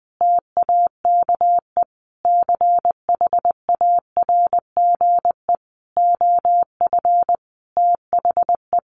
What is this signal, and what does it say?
Morse code, heard as a keyed tone, about 19 wpm, 700 Hz. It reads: TAKE CHARGE OF THE
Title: Sentences from Top 900 Words 20wpm